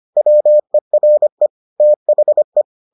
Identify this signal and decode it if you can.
Morse code, heard as a keyed tone, about 25 wpm, 600 Hz. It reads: WERE THE